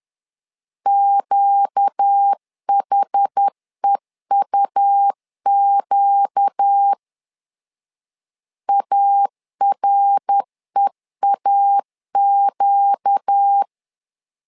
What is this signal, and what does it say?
Morse code, heard as a keyed tone, about 10 wpm, 800 Hz. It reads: QHEUQ AREAQ
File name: LEZIONI / 4 Q 10wpm 2210